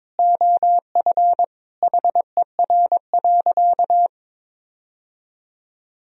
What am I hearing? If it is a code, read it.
Morse code, heard as a keyed tone, about 22 wpm, 700 Hz. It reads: OF HER.
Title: Sentences from Top 1000 Words - Review Part 3 - 22wpm